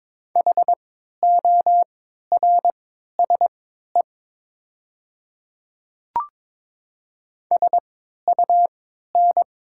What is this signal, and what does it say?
Morse code, heard as a keyed tone, about 21 wpm, spaced as for 12 wpm, 700 Hz. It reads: HORSE SUN
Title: 3x Character Spacing - Top 300 Words - 22wpm